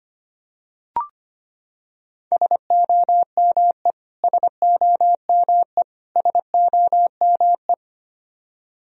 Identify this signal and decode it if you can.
Morse code, heard as a keyed tone, about 25 wpm, 700 Hz. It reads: SOME SOME SOME
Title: Top 100 Words Repeated 25wpm